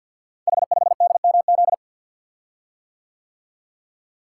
Morse code, heard as a keyed tone, about 46 wpm, 700 Hz. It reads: H5DMZ
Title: Code Groups - 5 Letter:Numbers - 50wpm